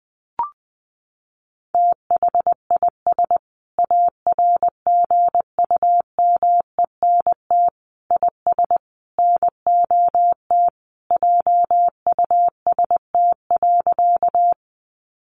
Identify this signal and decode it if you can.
Morse code, heard as a keyed tone, about 19 wpm, 700 Hz. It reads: THIS ARGUMENT IS NOT JUST.
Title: Sentences from Top 1400 Words 20wpm